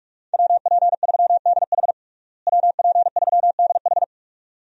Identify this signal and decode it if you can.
Morse code, heard as a keyed tone, about 42 wpm, 700 Hz. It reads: WP3DH WP3DH
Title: USA Call Signs - 5 Characters - Repeated 2x - 45wpm